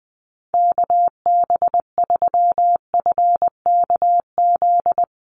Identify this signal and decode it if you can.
Morse code, heard as a keyed tone, about 20 wpm, 700 Hz. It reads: KB3FKZ
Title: USA Call Signs Repeated 3x 20wpm